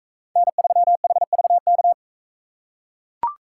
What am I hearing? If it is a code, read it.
Morse code, heard as a keyed tone, about 38 wpm, 700 Hz. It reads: N3HVK